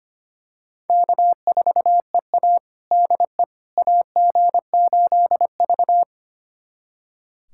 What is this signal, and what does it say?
Morse code, heard as a keyed tone, about 25 wpm, 700 Hz. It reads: K4EA DE AG8V
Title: USA Callsign DE Callsign - 4 Characters - 25wpm